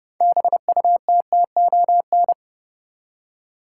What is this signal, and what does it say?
Morse code, heard as a keyed tone, about 30 wpm, 700 Hz. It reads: BUTTON
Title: Six Letter Words - 30wpm